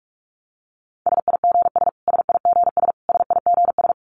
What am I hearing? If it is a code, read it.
Morse code, heard as a keyed tone, about 44 wpm, 700 Hz. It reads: SIGS SIGS SIGS
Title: Abbreviations Repeated 3x - 45wpm